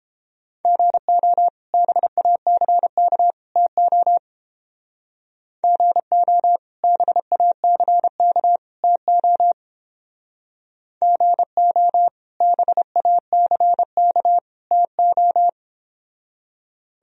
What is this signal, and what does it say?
Morse code, heard as a keyed tone, about 29 wpm, 700 Hz. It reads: GO BACK TO GO BACK TO GO BACK TO